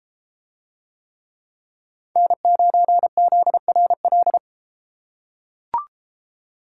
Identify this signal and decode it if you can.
Morse code, heard as a keyed tone, about 32 wpm, 700 Hz. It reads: N9ZRL